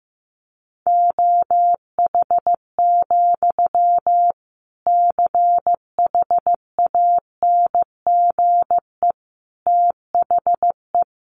Morse code, heard as a keyed tone, about 15 wpm, 700 Hz. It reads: OH, CHANGE THE
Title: Sentences from Top 800 Words 15wpm